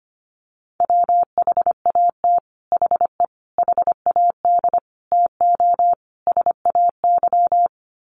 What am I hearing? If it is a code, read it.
Morse code, heard as a keyed tone, about 25 wpm, 700 Hz. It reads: WHAT HE HAD TO SAY
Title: Sets of 5 Words Encore2 25wpm